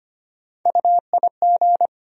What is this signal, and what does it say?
Morse code, heard as a keyed tone, about 25 wpm, 700 Hz. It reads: UIG
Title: USA Call Sign Postfixes - 3 Letters 25wpm